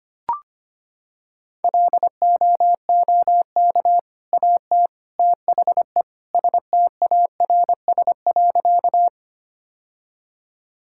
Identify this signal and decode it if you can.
Morse code, heard as a keyed tone, about 25 wpm, 700 Hz. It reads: LOOK AT THE STARS.
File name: practice / 25wpm / Sentences with 3-Word Phrases - 25wpm